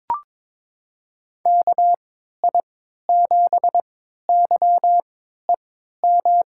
Morse code, heard as a keyed tone, about 22 wpm, spaced as for 13 wpm, 700 Hz. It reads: KI7YEM
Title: USA Call Signs - 6 Characters - 3x Character Spacing - 22wpm